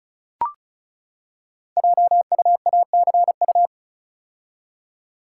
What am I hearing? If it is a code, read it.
Morse code, heard as a keyed tone, about 36 wpm, 690 Hz. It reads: JUACU
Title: Code Groups - 5 Letters - 35wpm